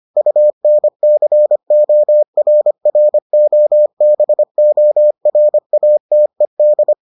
Morse code, heard as a keyed tone, about 25 wpm, 600 Hz. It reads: UNCORROBORATED